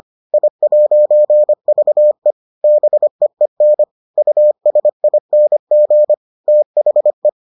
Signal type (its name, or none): Morse code